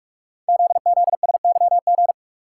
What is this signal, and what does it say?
Morse code, heard as a keyed tone, about 44 wpm, 700 Hz. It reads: ZZSYG
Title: Code Groups - 5 Letters - 45wpm